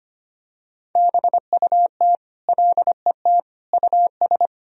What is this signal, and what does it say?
Morse code, heard as a keyed tone, about 25 wpm, 700 Hz. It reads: BUT LET US
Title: Sentences from Top 1000 Words - Review Part 1 - 25wpm